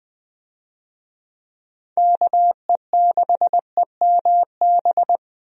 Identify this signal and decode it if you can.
Morse code, heard as a keyed tone, about 21 wpm, 700 Hz. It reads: KE6EMB
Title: USA Call Signs - 6 Characters - Repeated 2x - 20wpm